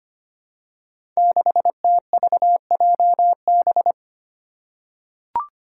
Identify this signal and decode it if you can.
Morse code, heard as a keyed tone, about 25 wpm, 700 Hz. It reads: 6TVJB
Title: Five Challenging Letter-Number Combinations 25wpm